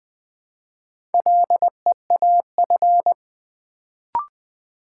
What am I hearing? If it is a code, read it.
Morse code, heard as a keyed tone, about 20 wpm, 700 Hz. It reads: LEAF
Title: Build A Word - 2 Letters At A Time - 4-Letter Words - 20wpm